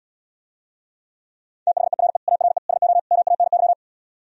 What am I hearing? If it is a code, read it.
Morse code, heard as a keyed tone, about 77 wpm, 700 Hz. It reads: THERE ARE SO MANY